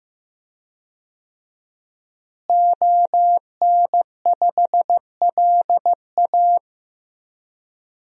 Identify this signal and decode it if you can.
Morse code, heard as a keyed tone, about 15 wpm, 700 Hz. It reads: ON5LA
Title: DXCC-100 Call Signs - 5 Characters - Repeated 3x - 15wpm